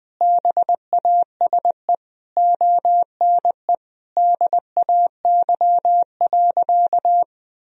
Morse code, heard as a keyed tone, about 21 wpm, 700 Hz. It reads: BASE ONE DAY.
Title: Sentences from Top 1000 Words - Review Part 1 - 20wpm